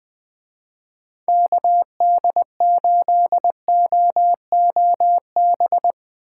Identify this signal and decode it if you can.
Morse code, heard as a keyed tone, about 24 wpm, 700 Hz. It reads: KD8OOB
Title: Speed Racing - USA 2x3 Call Signs 20wpm